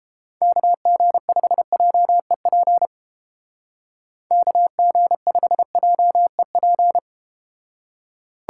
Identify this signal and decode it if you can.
Morse code, heard as a keyed tone, about 33 wpm, 700 Hz. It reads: KG5JEP KG5JEP